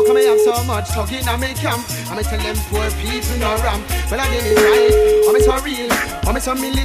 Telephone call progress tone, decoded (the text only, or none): ringback tone